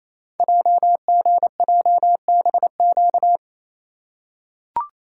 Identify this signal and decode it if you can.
Morse code, heard as a keyed tone, about 28 wpm, 700 Hz. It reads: JGJBQ
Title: Code Groups - 5 Letters - 28wpm